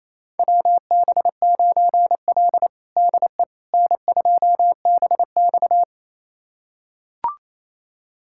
Morse code, heard as a keyed tone, about 28 wpm, 700 Hz. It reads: WB9L DE N2BX
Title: USA Callsign DE Callsign - 4 Characters - 28wpm